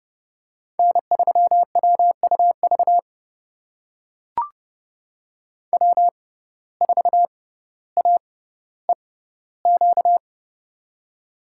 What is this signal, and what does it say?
Morse code, heard as a keyed tone, about 30 wpm, spaced as for 15 wpm, 700 Hz. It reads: N3WUV W4AEQ
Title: USA Call Signs - 5 Characters - 6x Character Spacing - 30wpm